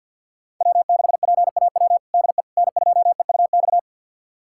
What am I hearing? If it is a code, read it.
Morse code, heard as a keyed tone, about 48 wpm, 700 Hz. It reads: W6PAW DE N2EUX
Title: USA Callsign DE Callsign - 5 Characters - 50wpm